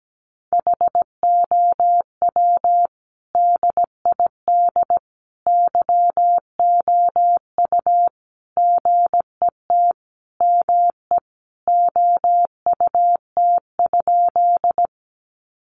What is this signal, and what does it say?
Morse code, heard as a keyed tone, about 17 wpm, 700 Hz. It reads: HOW DID YOU GET ME OUT?
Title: Sentences from Top 200 Words 17wpm